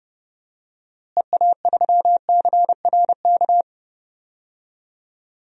Morse code, heard as a keyed tone, about 30 wpm, 700 Hz. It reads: EA3CRK
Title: DXCC-100 Call Signs - 6 Characters - Repeated 2x - 30wpm